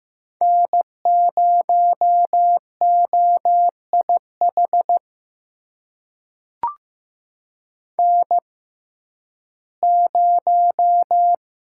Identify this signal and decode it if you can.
Morse code, heard as a keyed tone, about 15 wpm, spaced as for 11 wpm, 700 Hz. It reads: N0OIH N0
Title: USA Call Signs - 5 Characters - 6x Character Spacing - 15wpm